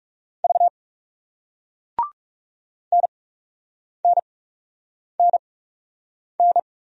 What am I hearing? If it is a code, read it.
Morse code, heard as a keyed tone, about 36 wpm, spaced as for 8 wpm, 700 Hz. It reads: V NNNN
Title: Speed-Racing Single Letters 30wpm